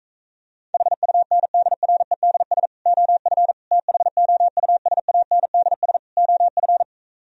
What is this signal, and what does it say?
Morse code, heard as a keyed tone, about 45 wpm, 700 Hz. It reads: HUNDREDS OF THOUSANDS OF